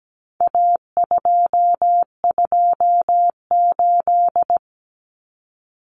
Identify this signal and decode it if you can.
Morse code, heard as a keyed tone, about 17 wpm, 700 Hz. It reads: A228
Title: Common Cut Numbers - 4-Digit 17wpm